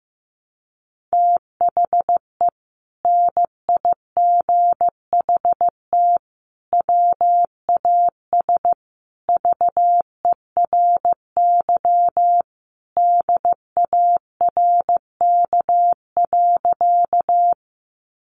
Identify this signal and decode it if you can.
Morse code, heard as a keyed tone, about 15 wpm, 700 Hz. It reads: THE NIGHT WAS VERY DARK.
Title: Sentences from Top 500 Words 15wpm